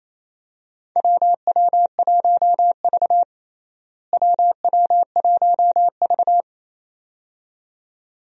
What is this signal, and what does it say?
Morse code, heard as a keyed tone, about 28 wpm, 700 Hz. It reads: WW1V WW1V
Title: USA Call Signs - 4 Characters - Repeated 2x - 28wpm